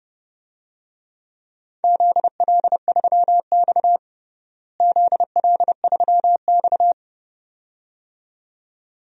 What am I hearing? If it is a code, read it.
Morse code, heard as a keyed tone, about 31 wpm, 700 Hz. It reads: ZL3X ZL3X
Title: DXCC-100 Call Signs - 4 Characters - Repeated 2x - 30wpm